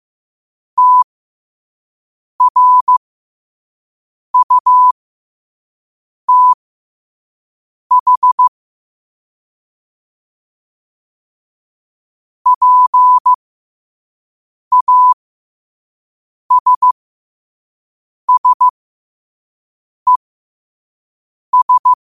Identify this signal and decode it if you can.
Morse code, heard as a keyed tone, about 15 wpm, spaced as for 5 wpm, 1000 Hz. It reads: TRUTH PASSES
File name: qotdfiles / 5wpm